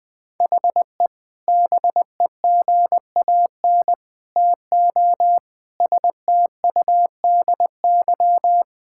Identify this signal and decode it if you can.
Morse code, heard as a keyed tone, about 20 wpm, 700 Hz. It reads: HE BEGAN TO STUDY
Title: Sentences from Top 1000 Words - Review Part 6 - 20wpm